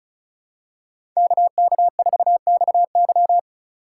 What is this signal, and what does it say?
Morse code, heard as a keyed tone, about 36 wpm, 700 Hz. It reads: KK4XY
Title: USA Call Signs - 5 Characters - 3x Character Spacing - 35wpm